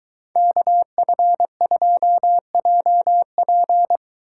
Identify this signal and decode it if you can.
Morse code, heard as a keyed tone, about 23 wpm, 700 Hz. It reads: KF2JP